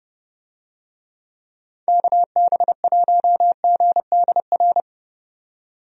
Morse code, heard as a keyed tone, about 30 wpm, 700 Hz. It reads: KB1GDR
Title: USA Call Signs - 6 Characters - 30wpm